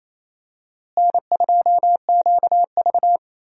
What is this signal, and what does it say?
Morse code, heard as a keyed tone, about 28 wpm, 700 Hz. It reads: N2QV
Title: USA Call Signs - 4 Characters - Repeated 2x - 28wpm